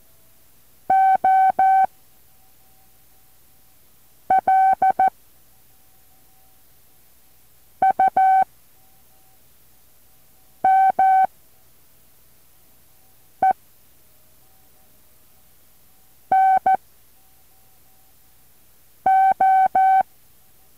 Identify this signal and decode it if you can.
Morse code, heard as a keyed tone, about 14 wpm, 760 Hz. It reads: O L U M E N O
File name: intl morse code2